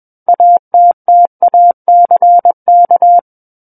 Morse code, heard as a keyed tone, about 21 wpm, 700 Hz. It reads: ATTACK
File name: Book One / 12-Chapter 12 - 21 WPM